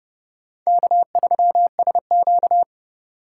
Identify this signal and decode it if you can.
Morse code, heard as a keyed tone, about 40 wpm, 700 Hz. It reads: K3SQ